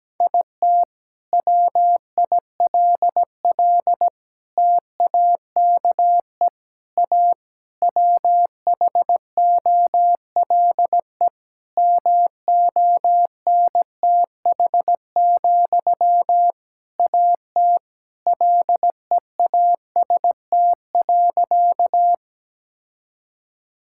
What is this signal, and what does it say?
Morse code, heard as a keyed tone, about 17 wpm, 700 Hz. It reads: IT WILL TAKE A WHOLE MONTH, AT LEAST.
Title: Sentences from Top 1000 Words - Review Part 5 - 17wpm